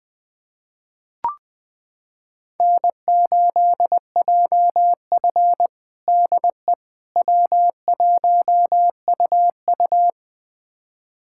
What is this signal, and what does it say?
Morse code, heard as a keyed tone, about 20 wpm, 700 Hz. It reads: N8JF DE W1UU
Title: USA Callsign DE Callsign - 4 Characters - 20wpm